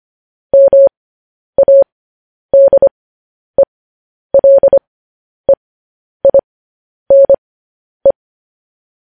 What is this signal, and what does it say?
Morse code, heard as a keyed tone, about 25 wpm, spaced as for 10 wpm, 550 Hz. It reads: MADELEINE
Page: news-headlines-in-morse-code-at-10-wpm